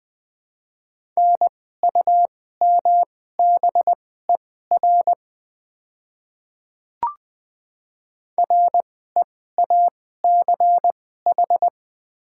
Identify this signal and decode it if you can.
Morse code, heard as a keyed tone, about 21 wpm, spaced as for 15 wpm, 700 Hz. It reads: NUMBER REACH
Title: 2x Character Spacing - Top 500 Words - 20wpm